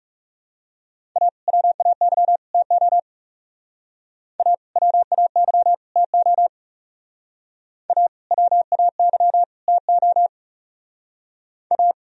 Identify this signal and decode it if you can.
Morse code, heard as a keyed tone, about 40 wpm, 700 Hz. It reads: A WAY TO A WAY TO A WAY TO A